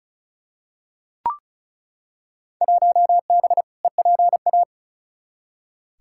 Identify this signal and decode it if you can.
Morse code, heard as a keyed tone, about 36 wpm, 700 Hz. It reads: 1B EPA
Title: Field Day Exchanges - 35wpm